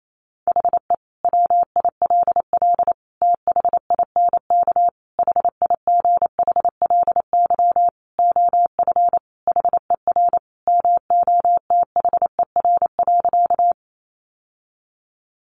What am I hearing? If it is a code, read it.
Morse code, heard as a keyed tone, about 28 wpm, 700 Hz. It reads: HE WILL THINK HIGHLY OF HER MOTHER.
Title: Sentences from Top 1100 Words 28wpm